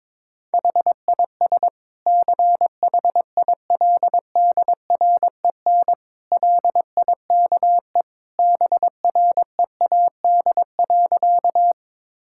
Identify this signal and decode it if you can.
Morse code, heard as a keyed tone, about 22 wpm, 700 Hz. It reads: HIS CHILDREN LIKE BREAD.
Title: Sentences from Top 1600 Words 22wpm